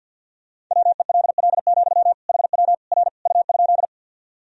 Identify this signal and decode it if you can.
Morse code, heard as a keyed tone, about 48 wpm, 700 Hz. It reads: WELL, HW R U?
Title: Sentences from Top 200 Words and Abbreviations 50wpm